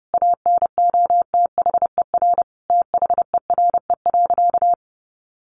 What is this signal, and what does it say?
Morse code, heard as a keyed tone, about 30 wpm, 700 Hz. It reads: ANOTHER THERE.
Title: Sentences from Top 1400 Words 30wpm